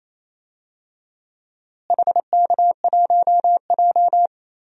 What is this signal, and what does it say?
Morse code, heard as a keyed tone, about 28 wpm, 700 Hz. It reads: HK1J
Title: DXCC-100 Call Signs - 4 Characters - Repeated 2x - 28wpm